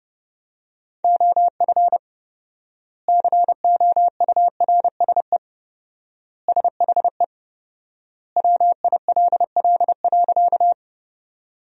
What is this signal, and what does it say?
Morse code, heard as a keyed tone, about 32 wpm, 700 Hz. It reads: OF COURSE SHE WILL.
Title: 4x Word-Spacing - Sentences from Top 500 Words - 30wpm